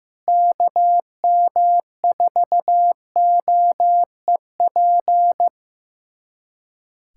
Morse code, heard as a keyed tone, about 15 wpm, 700 Hz. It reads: KM4OEP
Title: USA Callsign DE Callsign - 6 Characters - 15wpm